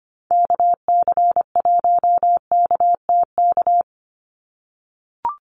Morse code, heard as a keyed tone, about 25 wpm, 700 Hz. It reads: KC1KTK
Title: USA Call Signs 25wpm